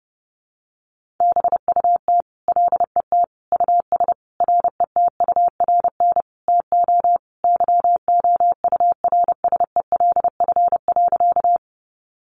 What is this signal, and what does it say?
Morse code, heard as a keyed tone, about 30 wpm, 700 Hz. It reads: BUT LET US RETURN TO YOURSELF.
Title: Sentences from Top 800 Words 30wpm